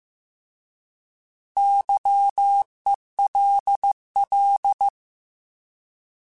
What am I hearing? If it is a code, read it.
Morse code, heard as a keyed tone, about 15 wpm, 800 Hz. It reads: YELL